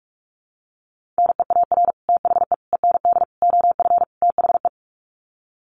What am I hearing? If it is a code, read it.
Morse code, heard as a keyed tone, about 44 wpm, 700 Hz. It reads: NEAR THE END OF THE
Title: Sets of 5 Words Encore 45wpm